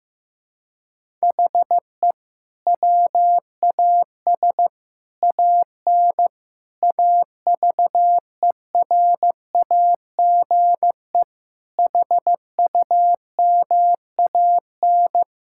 Morse code, heard as a keyed tone, about 15 wpm, 700 Hz. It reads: HE WAS AN AVERAGE HUMAN
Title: Sentences from Top 1000 Words - Review Part 6 - 15wpm